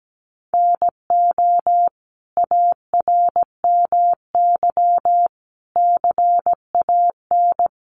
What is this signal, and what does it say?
Morse code, heard as a keyed tone, about 17 wpm, 700 Hz. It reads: NO ARMY CAN
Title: Sentences from Top 1300 Words 17wpm